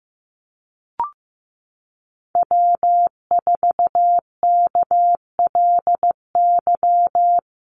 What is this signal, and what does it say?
Morse code, heard as a keyed tone, about 14 wpm, 700 Hz. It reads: W4KLY
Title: Call Signs Repeated 2x - 15wpm